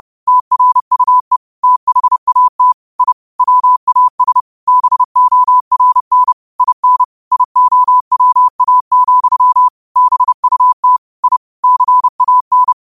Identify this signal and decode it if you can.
Morse code, heard as a keyed tone, about 30 wpm, 1000 Hz. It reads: TRUE THAT I WAS BORN IN IOWA, BUT I CAN